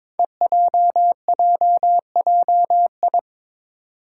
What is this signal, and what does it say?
Morse code, heard as a keyed tone, about 22 wpm, 700 Hz. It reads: EJJJI
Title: Five Challenging Letter-Number Combinations 22wpm